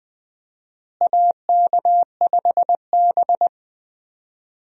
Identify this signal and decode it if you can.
Morse code, heard as a keyed tone, about 20 wpm, 700 Hz. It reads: AK5B
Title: USA Call Signs - 4 Characters - Repeated 2x - 20wpm